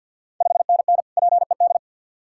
Morse code, heard as a keyed tone, about 50 wpm, 690 Hz. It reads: 5NN PED
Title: Ontario QSO Party Exchange 50wpm